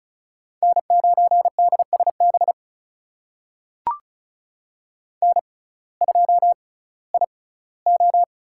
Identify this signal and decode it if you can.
Morse code, heard as a keyed tone, about 34 wpm, spaced as for 19 wpm, 700 Hz. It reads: N9DSB N2IO
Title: USA Call Signs - 5 Characters - 6x Character Spacing - 35wpm